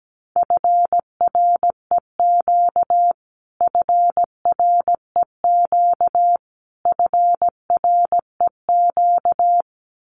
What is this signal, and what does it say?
Morse code, heard as a keyed tone, about 17 wpm, 700 Hz. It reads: FREQ FREQ FREQ
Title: Abbreviations Repeated 3x - 17wpm